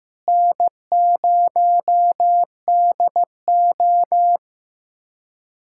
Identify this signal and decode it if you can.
Morse code, heard as a keyed tone, about 15 wpm, 700 Hz. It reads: N0DO
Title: USA Callsign DE Callsign - 4 Characters - 15wpm